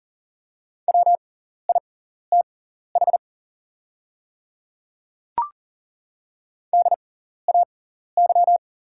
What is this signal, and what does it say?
Morse code, heard as a keyed tone, about 40 wpm, spaced as for 14 wpm, 700 Hz. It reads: WITH DAY
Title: 6x Character Spacing - Top 100 Words - 40wpm